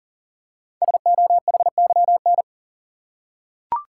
Morse code, heard as a keyed tone, about 39 wpm, 690 Hz. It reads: SOHYN
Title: Code Groups - 5 Letters - 40wpm